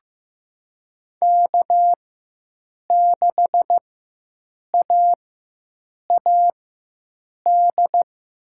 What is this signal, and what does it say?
Morse code, heard as a keyed tone, about 15 wpm, spaced as for 7 wpm, 700 Hz. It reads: K6AAD
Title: USA Call Signs - 5 Characters - 4x Character Spacing - 15wpm